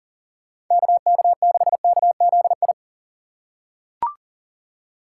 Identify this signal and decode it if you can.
Morse code, heard as a keyed tone, about 40 wpm, 700 Hz. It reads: KK6KZI
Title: USA Call Signs - 6 Characters - 2x Character Spacing - 40wpm